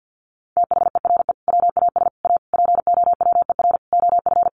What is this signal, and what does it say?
Morse code, heard as a keyed tone, about 50 wpm, 700 Hz. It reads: THERE WAS A POWER OF
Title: Sentences from Top 800 Words 50wpm